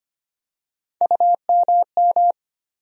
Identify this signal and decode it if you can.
Morse code, heard as a keyed tone, about 25 wpm, 700 Hz. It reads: UMM
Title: USA Call Sign Postfixes - 3 Letters 25wpm